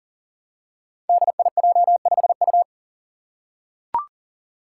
Morse code, heard as a keyed tone, about 38 wpm, 700 Hz. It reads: DIJ5U
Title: Five Challenging Letter-Number Combinations 40wpm